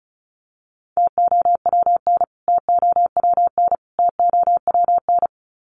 Morse code, heard as a keyed tone, about 34 wpm, 700 Hz. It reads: TOWN TOWN TOWN
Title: Top 500 Words Repeated 35wpm